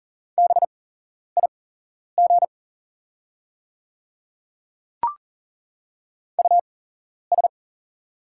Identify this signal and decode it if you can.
Morse code, heard as a keyed tone, about 40 wpm, spaced as for 11 wpm, 700 Hz. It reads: BIG US
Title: 8x Character Spacing - Top 200 Words - 40wpm